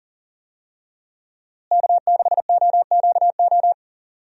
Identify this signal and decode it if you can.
Morse code, heard as a keyed tone, about 37 wpm, 700 Hz. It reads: K6OQO
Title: USA Call Signs - 5 Characters - Repeated 3x - 40wpm